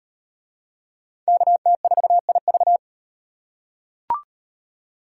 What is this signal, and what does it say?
Morse code, heard as a keyed tone, about 37 wpm, 700 Hz. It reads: KT4IV